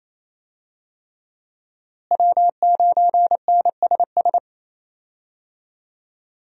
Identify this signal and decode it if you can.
Morse code, heard as a keyed tone, about 28 wpm, 700 Hz. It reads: W9NSS
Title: USA Call Signs - 5 Characters - Repeated 3x - 28wpm